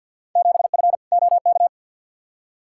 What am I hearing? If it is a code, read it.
Morse code, heard as a keyed tone, about 47 wpm, 700 Hz. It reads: 7F OK
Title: Field Day Exchanges Encore - 50wpm